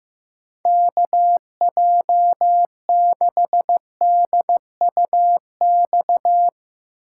Morse code, heard as a keyed tone, about 15 wpm, 700 Hz. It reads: KJ6DUX
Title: USA Call Signs - 6 Characters - 3x Character Spacing - 15wpm